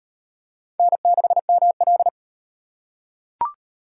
Morse code, heard as a keyed tone, about 37 wpm, 700 Hz. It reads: N6ML